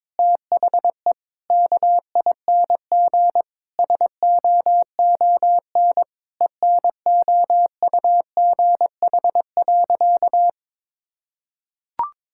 Morse code, heard as a keyed tone, about 22 wpm, 700 Hz. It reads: THE KING SOON ENOUGH.
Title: Sentences from Top 1000 Words - Review Part 6 - 22wpm